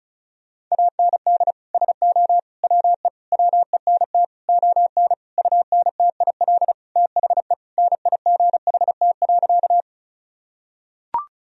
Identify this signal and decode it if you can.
Morse code, heard as a keyed tone, about 36 wpm, 700 Hz. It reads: AND SO WE WENT ON UNTIL THE NIGHT.
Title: Sentences from Top 1000 Words - Review Part 1 - 35wpm